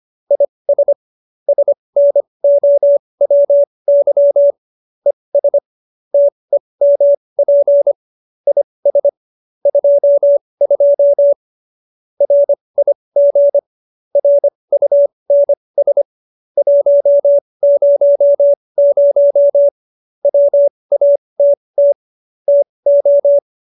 Morse code, heard as a keyed tone, about 25 wpm, spaced as for 20 wpm, 570 Hz. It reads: IS SNOWY ES TEMP IS 22 RIG RUNS 100 WATT TO